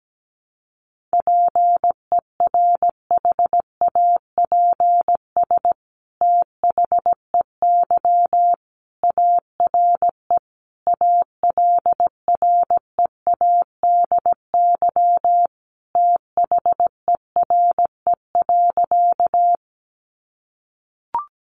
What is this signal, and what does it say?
Morse code, heard as a keyed tone, about 17 wpm, 700 Hz. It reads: PERHAPS THEY ARE ALREADY THERE.
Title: Sentences from Top 700 Words 17wpm